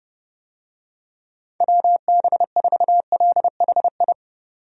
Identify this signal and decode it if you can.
Morse code, heard as a keyed tone, about 31 wpm, 700 Hz. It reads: WB4LHI
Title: USA Call Signs - 6 Characters - 30wpm